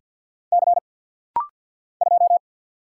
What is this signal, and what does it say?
Morse code, heard as a keyed tone, about 50 wpm, 700 Hz. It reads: C 2
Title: Single Letter-Number - Rapid-Fire 50wpm